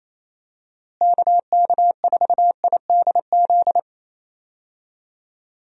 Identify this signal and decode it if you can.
Morse code, heard as a keyed tone, about 28 wpm, 700 Hz. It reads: KK4IDZ
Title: USA Call Signs - 6 Characters - Repeated 2x - 28wpm